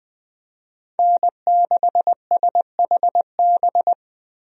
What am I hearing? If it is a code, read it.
Morse code, heard as a keyed tone, about 20 wpm, 700 Hz. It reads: N6SHB
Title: USA Call Signs - 5 Characters - Repeated 2x - 20wpm